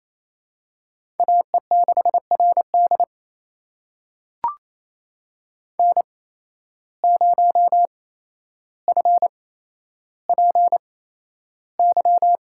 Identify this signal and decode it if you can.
Morse code, heard as a keyed tone, about 27 wpm, spaced as for 12 wpm, 700 Hz. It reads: AE6RD N0FPY
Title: USA Call Signs - 5 Characters - 8x Character Spacing - 28wpm